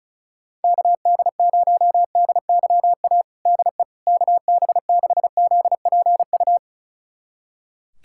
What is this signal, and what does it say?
Morse code, heard as a keyed tone, about 34 wpm, 700 Hz. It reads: KD0DYA DE KB6ZPU
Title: USA Callsign DE Callsign 35wpm